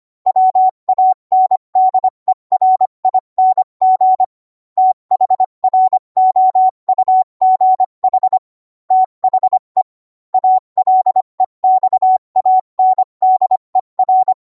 Morse code, heard as a keyed tone, about 25 wpm, 750 Hz. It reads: WANDERING THROUGH THE ALEXANDER